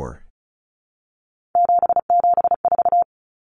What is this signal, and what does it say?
Morse code, heard as a keyed tone, about 34 wpm, 700 Hz. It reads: 774
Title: Common Cut Numbers - 3-Digit 35wpm